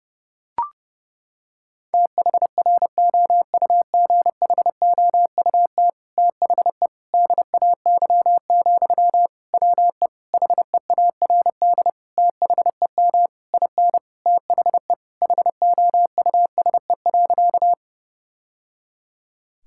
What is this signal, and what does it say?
Morse code, heard as a keyed tone, about 30 wpm, 700 Hz. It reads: THROUGHOUT THE DAY, WE HEARD THEM IN THE HOUSE.
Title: Sentences from Top 1000 Words - Review Part 5 - 30wpm